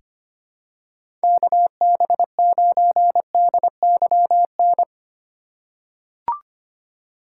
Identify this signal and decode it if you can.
Morse code, heard as a keyed tone, about 25 wpm, 700 Hz. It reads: KB9DYN
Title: USA Call Signs - 4x Character Spacing - 25wpm